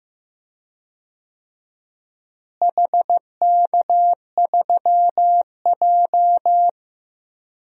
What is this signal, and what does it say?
Morse code, heard as a keyed tone, about 15 wpm, 700 Hz. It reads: HK3J
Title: DXCC-100 Call Signs - 4 Characters - Repeated 2x - 15wpm